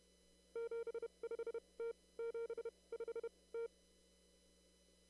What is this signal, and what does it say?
Morse code, heard as a keyed tone, about 32 wpm, 460 Hz. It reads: 7 5 T 7 5 T
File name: CW / kt5j wpx cw-op-k5pi